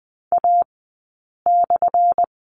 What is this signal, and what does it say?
Morse code, heard as a keyed tone, about 19 wpm, spaced as for 8 wpm, 700 Hz. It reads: A/
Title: 190 Lesson 31d - Letter v - Two Characters 20wpm